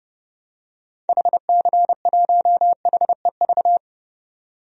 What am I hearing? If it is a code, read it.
Morse code, heard as a keyed tone, about 30 wpm, 700 Hz. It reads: HC1HEV
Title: DXCC-100 Call Signs - 6 Characters - 30wpm